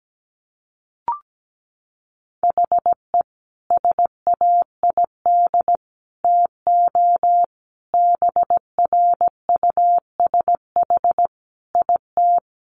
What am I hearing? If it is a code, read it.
Morse code, heard as a keyed tone, about 17 wpm, 700 Hz. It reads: HE SAID TO BRUSH IT
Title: Sentences from Top 1100 Words 17wpm